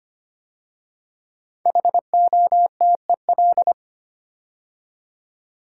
Morse code, heard as a keyed tone, about 25 wpm, 700 Hz. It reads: HOTEL
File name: 5 Letter Words - First 3 Letters Given - 25wpm